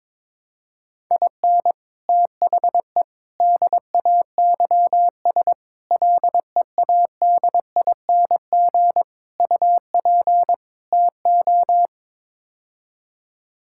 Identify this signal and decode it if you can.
Morse code, heard as a keyed tone, about 22 wpm, 700 Hz. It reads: IN THE DAYS LEADING UP TO
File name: practice / 22wpm / Sets of 6 Words - 22wpm